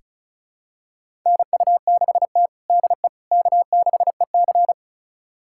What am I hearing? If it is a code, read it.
Morse code, heard as a keyed tone, about 36 wpm, 700 Hz. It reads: NU6T DE K6EC
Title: USA Callsign DE Callsign - 4 Characters - 35wpm